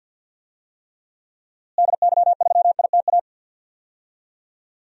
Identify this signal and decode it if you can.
Morse code, heard as a keyed tone, about 51 wpm, 700 Hz. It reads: DY3ITA
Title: DXCC-100 Call Signs - 6 Characters - Repeated 3x - 50wpm